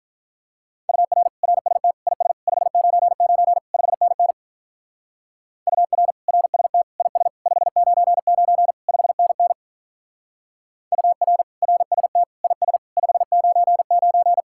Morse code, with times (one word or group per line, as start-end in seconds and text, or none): 0.9-1.3 UR
1.4-1.9 RST
2.1-2.3 IS
2.5-3.6 599
3.7-4.3 5NN
5.7-6.1 UR
6.3-6.8 RST
7.0-7.3 IS
7.5-8.7 599
8.9-9.5 5NN
10.9-11.4 UR
11.6-12.2 RST
12.4-12.8 IS
13.0-14.4 599